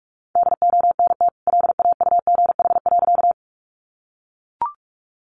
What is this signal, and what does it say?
Morse code, heard as a keyed tone, about 44 wpm, 700 Hz. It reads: DONT LAUGH.
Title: Sentences from Top 500 Words 45wpm